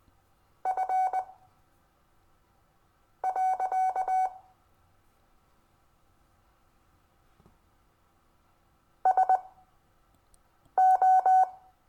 Morse code, heard as a keyed tone, about 21 wpm, spaced as for 5 wpm, 750 Hz. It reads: F. SO
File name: cw / Lesson22 REVIEW